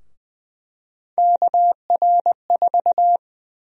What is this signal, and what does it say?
Morse code, heard as a keyed tone, about 20 wpm, 700 Hz. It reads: KR4
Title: USA Call Sign Prefixes - 20wpm